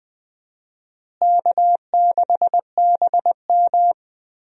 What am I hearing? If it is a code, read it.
Morse code, heard as a keyed tone, about 21 wpm, 700 Hz. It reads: K6BM
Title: USA Call Signs - 4 Characters - Repeated 2x - 20wpm